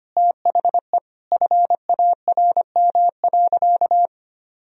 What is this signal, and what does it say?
Morse code, heard as a keyed tone, about 25 wpm, 700 Hz. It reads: THE FARM.
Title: Sentences from Top 1000 Words - Review Part 4 - 25wpm